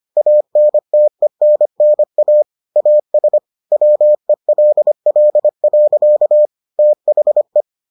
Morse code, heard as a keyed tone, about 25 wpm, 600 Hz. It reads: ANTENNA AS WELL. THE